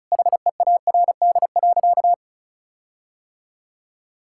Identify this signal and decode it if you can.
Morse code, heard as a keyed tone, about 36 wpm, 700 Hz. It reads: HEARD.
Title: Sentences from Top 1600 Words 35wpm